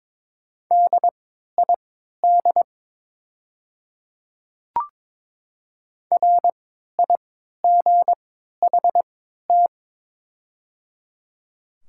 Morse code, heard as a keyed tone, about 23 wpm, spaced as for 13 wpm, 700 Hz. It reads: DID RIGHT
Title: 3x Character Spacing - Top 200 Words - 22wpm